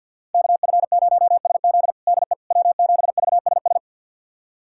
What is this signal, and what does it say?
Morse code, heard as a keyed tone, about 51 wpm, 700 Hz. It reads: KF0SZ DE W7VSS
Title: USA Callsign DE Callsign - 5 Characters - 50wpm